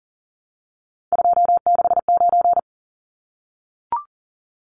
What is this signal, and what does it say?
Morse code, heard as a keyed tone, about 40 wpm, 700 Hz. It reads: 269
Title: Numbers - 3-Digit 40wpm